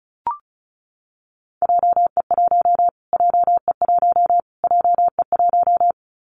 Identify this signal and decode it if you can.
Morse code, heard as a keyed tone, about 34 wpm, 700 Hz. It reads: JE1 JE1 JE1
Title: Callsign Prefixes for Top 25 DXCC - Repeated 3x - 35wpm